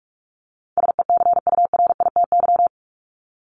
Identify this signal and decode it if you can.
Morse code, heard as a keyed tone, about 44 wpm, 700 Hz. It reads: SECURITY
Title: Top 501-750 Words 45wpm